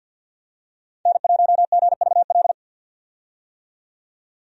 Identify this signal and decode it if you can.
Morse code, heard as a keyed tone, about 50 wpm, 700 Hz. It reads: N1GVL
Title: USA Call Signs - 5 Characters - 50wpm